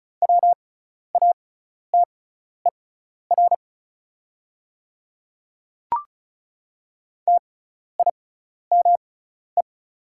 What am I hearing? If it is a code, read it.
Morse code, heard as a keyed tone, about 36 wpm, spaced as for 12 wpm, 700 Hz. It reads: WATER TIME
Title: 6x Character Spacing - Top 100 Words - 35wpm